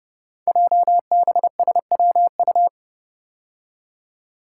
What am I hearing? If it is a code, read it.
Morse code, heard as a keyed tone, about 30 wpm, 700 Hz. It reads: JBSWU
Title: Code Groups - 5 Letter:Numbers - 30wpm